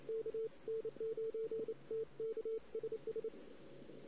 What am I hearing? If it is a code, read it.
Morse code, heard as a keyed tone, about 29 wpm, 450 Hz. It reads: K N9TK SS
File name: CW / 2008-ss-cw-k5tr-op-n5rz -16:00